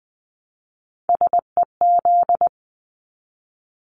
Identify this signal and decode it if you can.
Morse code, heard as a keyed tone, about 20 wpm, 700 Hz. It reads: SEZ
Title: Abbreviations Repeated 3x - 20wpm